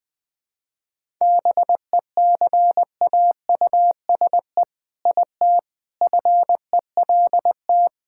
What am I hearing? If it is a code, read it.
Morse code, heard as a keyed tone, about 21 wpm, 700 Hz. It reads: BECAUSE IT FELT
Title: All-Age Appropriate Jokes - 20wpm